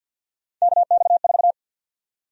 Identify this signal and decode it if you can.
Morse code, heard as a keyed tone, about 50 wpm, 700 Hz. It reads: KX4
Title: USA Call Sign Prefixes - Repeated 2x - 50wpm